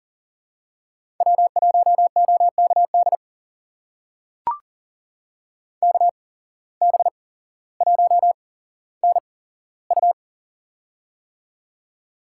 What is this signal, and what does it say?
Morse code, heard as a keyed tone, about 40 wpm, spaced as for 17 wpm, 700 Hz. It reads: W1OKD KB1NU
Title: USA Call Signs - 5 Characters - 8x Character Spacing - 40wpm